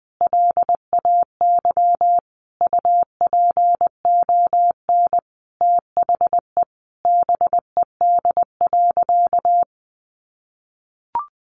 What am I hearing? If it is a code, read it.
Morse code, heard as a keyed tone, about 19 wpm, 700 Hz. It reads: LAY UPON THE BED.
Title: Sentences from Top 1100 Words 20wpm